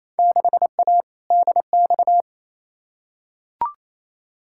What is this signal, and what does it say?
Morse code, heard as a keyed tone, about 28 wpm, 700 Hz. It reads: 6A DX
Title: Field Day Exchanges Encore - 28wpm